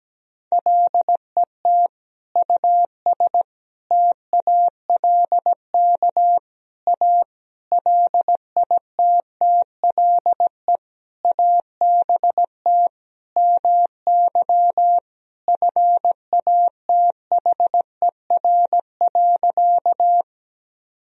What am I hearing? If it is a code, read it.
Morse code, heard as a keyed tone, about 17 wpm, 700 Hz. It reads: LET US TALK A LITTLE ABT MY FATHER.
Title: Sentences from Top 400 Words and Abbreviations 17wpm